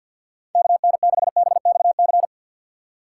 Morse code, heard as a keyed tone, about 64 wpm, 700 Hz. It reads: KN6BXC